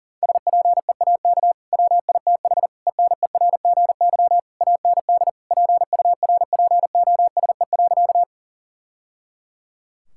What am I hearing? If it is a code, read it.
Morse code, heard as a keyed tone, about 40 wpm, 700 Hz. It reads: SPEAK WITH ENERGY AND PURPOSE.
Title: Sentences from Top 1000 Words - Review Part 6 - 40wpm